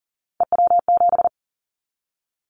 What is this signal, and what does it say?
Morse code, heard as a keyed tone, about 42 wpm, 700 Hz. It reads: EW7
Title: Callsign Prefixes for Top 100 DXCC - 40wpm